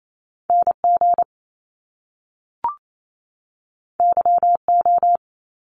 Morse code, heard as a keyed tone, about 28 wpm, 700 Hz. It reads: NG YO
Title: Top Two Letter Combinations in Words 28wpm